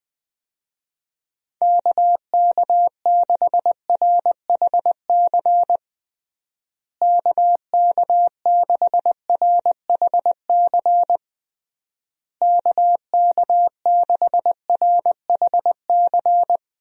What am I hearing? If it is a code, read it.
Morse code, heard as a keyed tone, about 20 wpm, 700 Hz. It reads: KK6RHC KK6RHC KK6RHC
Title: USA Call Signs - 6 Characters - Repeated 3x - 20wpm